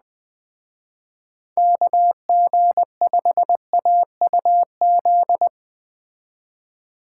Morse code, heard as a keyed tone, about 20 wpm, 700 Hz. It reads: KG5AUZ
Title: Speed Racing - USA 2x3 Call Signs 17wpm